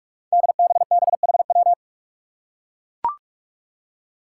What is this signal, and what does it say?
Morse code, heard as a keyed tone, about 42 wpm, 700 Hz. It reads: DBBHW
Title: Five Challenging Letter-Number Combinations 45wpm